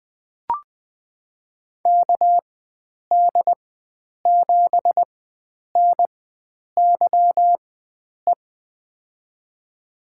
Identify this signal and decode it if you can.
Morse code, heard as a keyed tone, about 20 wpm, spaced as for 9 wpm, 700 Hz. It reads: KD7NYE
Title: USA Call Signs - 6 Characters - 4x Character Spacing - 20wpm